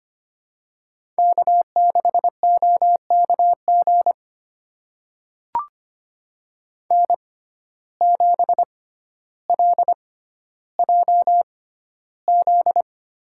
Morse code, heard as a keyed tone, about 25 wpm, spaced as for 13 wpm, 700 Hz. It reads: K6OKG N7LJZ
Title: USA Call Signs - 5 Characters - 6x Character Spacing - 25wpm